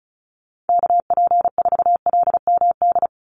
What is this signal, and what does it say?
Morse code, heard as a keyed tone, about 36 wpm, 700 Hz. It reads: KP4LMD
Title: USA Call Signs 35wpm